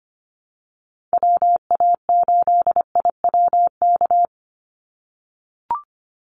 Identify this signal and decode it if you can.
Morse code, heard as a keyed tone, about 25 wpm, 700 Hz. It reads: WA8IWK
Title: Call Signs - 25wpm